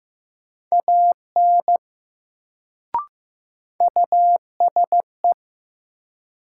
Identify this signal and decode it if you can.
Morse code, heard as a keyed tone, about 15 wpm, 700 Hz. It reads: AN USE
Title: Sets of 2 Words - New Word Drill - 15wpm